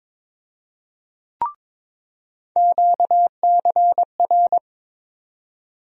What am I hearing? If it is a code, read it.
Morse code, heard as a keyed tone, about 22 wpm, 700 Hz. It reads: QCR
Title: USA Call Sign Postfixes - 3 Letters 22wpm